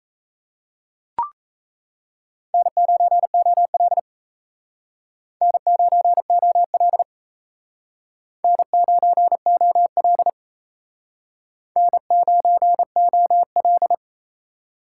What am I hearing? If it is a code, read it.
Morse code, heard as a keyed tone, about 34 wpm, 700 Hz. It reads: N9OL N9OL N9OL N9OL
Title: Speed Racing - USA 1x2 Call Signs 28wpm